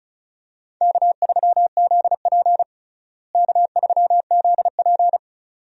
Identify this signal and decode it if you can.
Morse code, heard as a keyed tone, about 36 wpm, 700 Hz. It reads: K3ZP K3ZP
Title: USA Call Signs - 4 Characters - Repeated 2x - 35wpm